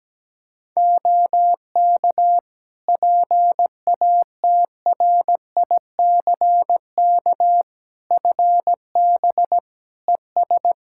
Morse code, heard as a keyed tone, about 17 wpm, 700 Hz. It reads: OK PATRICK FB ES
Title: Practice QSO - Exchange 2 - Info - 17wpm